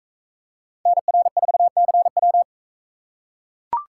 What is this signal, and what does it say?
Morse code, heard as a keyed tone, about 40 wpm, 700 Hz. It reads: NR4CW